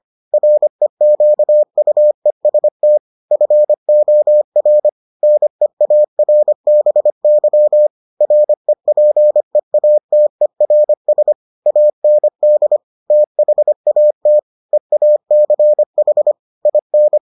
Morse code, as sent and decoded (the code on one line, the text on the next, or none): .-. --.- ..- . ... - / ..-. --- .-. / -. .- .-. -... -.-- / .-. .--. .- - . .-. ... / .- -. -.. / - .... .- - / . .- -.-. .... / .. -.
REQUEST FOR NEARBY REPEATERS AND THAT EACH IN